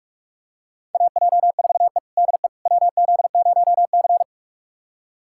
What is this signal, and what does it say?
Morse code, heard as a keyed tone, about 42 wpm, 700 Hz. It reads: AJ4E DE WZ0C
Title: USA Callsign DE Callsign - 4 Characters - 45wpm